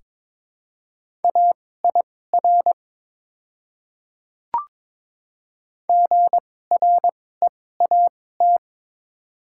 Morse code, heard as a keyed tone, about 22 wpm, spaced as for 16 wpm, 700 Hz. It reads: AIR GREAT
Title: 2x Character Spacing - Top 200 Words - 22wpm